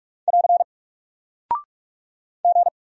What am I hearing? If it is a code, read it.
Morse code, heard as a keyed tone, about 44 wpm, 700 Hz. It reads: <AR> G
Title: Single Character (Letter:Number:Punctuation:Prosign) Alt 45wpm